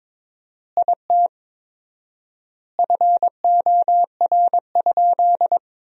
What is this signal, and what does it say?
Morse code, heard as a keyed tone, about 22 wpm, 700 Hz. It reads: IT FOR?
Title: 4x Word-Spacing - Sentences from Top 100 Words - 22wpm